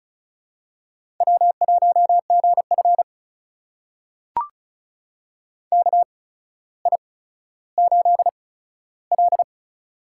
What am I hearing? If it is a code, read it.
Morse code, heard as a keyed tone, about 36 wpm, spaced as for 15 wpm, 700 Hz. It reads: W1GF KI8L
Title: USA Call Signs - 4 Characters - 8x Character Spacing - 35wpm